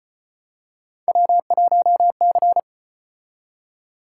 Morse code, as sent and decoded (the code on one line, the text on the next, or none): .-- .---- -.-.
W1C